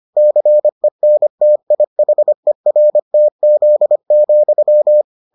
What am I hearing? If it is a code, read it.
Morse code, heard as a keyed tone, about 25 wpm, 600 Hz. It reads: CENTIHERTZ,